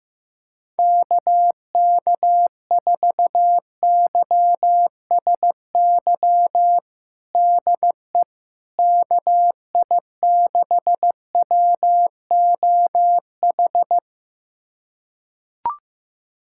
Morse code, heard as a keyed tone, about 15 wpm, 700 Hz. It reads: KK4YSY DE KI6WOH
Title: USA Callsign DE Callsign 15wpm